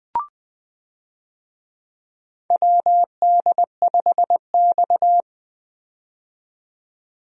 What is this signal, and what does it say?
Morse code, heard as a keyed tone, about 21 wpm, 700 Hz. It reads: WD5X
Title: USA Call Signs - 4 Characters - 20wpm